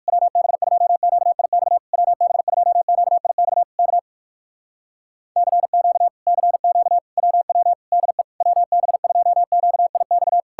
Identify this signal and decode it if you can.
Morse code, heard as a keyed tone, about 59 wpm, 700 Hz. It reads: WB2QIX WB2QIX K CQ CQ WW DE WB2QIX